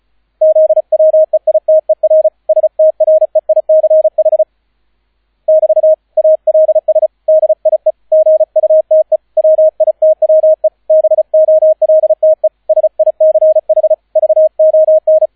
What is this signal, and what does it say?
Morse code, heard as a keyed tone, about 34 wpm, 630 Hz, with radio noise. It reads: ZWEITER STREICH <BT> ALS DIE GUTE WITWE BOLTE SICH VON